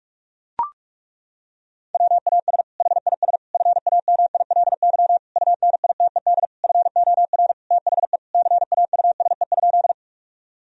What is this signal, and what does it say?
Morse code, heard as a keyed tone, about 41 wpm, 700 Hz. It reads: WAS HIS FAMILY UNITED FOR THE CAUSE?
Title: Sentences from Top 1000 Words - Review Part 4 - 45wpm